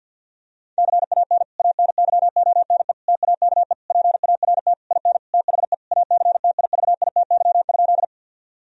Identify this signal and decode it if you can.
Morse code, heard as a keyed tone, about 50 wpm, 700 Hz. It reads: CAN ANYONE TAKE PART IN THE ACTIVITY?
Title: Sentences from Top 1000 Words - Review Part 4 - 50wpm